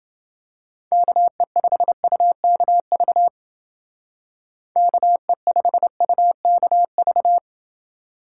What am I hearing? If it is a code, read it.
Morse code, heard as a keyed tone, about 30 wpm, 700 Hz. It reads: KE5UKV KE5UKV